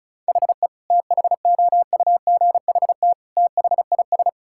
Morse code, heard as a keyed tone, about 36 wpm, 700 Hz. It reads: HE THOUGHT THIS
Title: Sentences from Top 1600 Words 35wpm